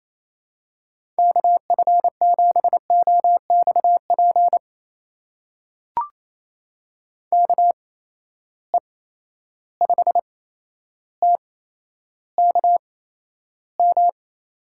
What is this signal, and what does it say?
Morse code, heard as a keyed tone, about 28 wpm, spaced as for 12 wpm, 700 Hz. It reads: KF7OXP KE5TKM
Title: USA Call Signs - 6 Characters - 8x Character Spacing - 28wpm